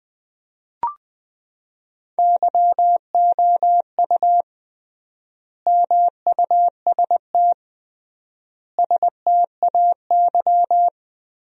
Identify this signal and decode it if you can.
Morse code, heard as a keyed tone, about 20 wpm, 700 Hz. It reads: YOU MUST STAY
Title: 3x Word-Spacing - Sentences from Top 500 Words - 20wpm